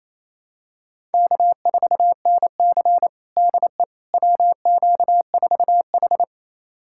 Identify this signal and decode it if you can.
Morse code, heard as a keyed tone, about 28 wpm, 700 Hz. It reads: K4NC DE WQ4H
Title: USA Callsign DE Callsign - 4 Characters - 28wpm